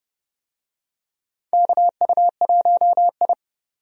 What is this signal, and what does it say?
Morse code, heard as a keyed tone, about 29 wpm, 700 Hz. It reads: KU1I
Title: USA Call Signs - 4 Characters - 30wpm